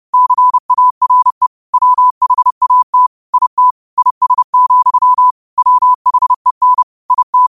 Morse code, heard as a keyed tone, about 30 wpm, 1000 Hz. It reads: CARE WHAT IT IS, WHEN IT
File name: qotdfiles / 30wpm